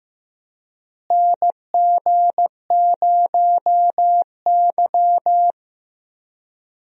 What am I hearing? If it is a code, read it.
Morse code, heard as a keyed tone, about 15 wpm, 700 Hz. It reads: NG0Y
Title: USA Call Signs - 4 Characters - Repeated 2x - 15wpm